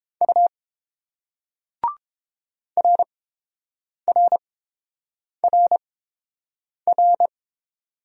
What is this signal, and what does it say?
Morse code, heard as a keyed tone, about 30 wpm, spaced as for 7 wpm, 690 Hz. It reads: U RRRR